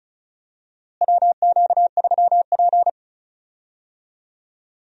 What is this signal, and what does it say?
Morse code, heard as a keyed tone, about 34 wpm, 700 Hz. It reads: WQ3P
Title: USA Call Signs - 4 Characters - Repeated 3x - 35wpm